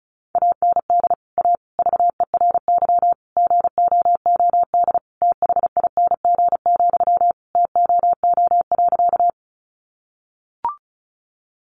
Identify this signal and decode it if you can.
Morse code, heard as a keyed tone, about 34 wpm, 700 Hz. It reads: AND A VERY GOOD THING, TOO.
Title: Sentences from Top 200 Words 35wpm